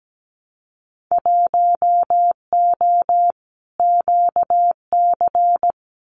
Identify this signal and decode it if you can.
Morse code, heard as a keyed tone, about 17 wpm, 700 Hz. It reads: 1O QC
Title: Winter Field Day Exchanges 17wpm